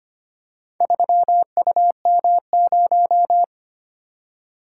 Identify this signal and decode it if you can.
Morse code, heard as a keyed tone, about 25 wpm, 700 Hz. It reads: 3UM0
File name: practice / 25wpm / Four Letter-Number Combinations - 25wpm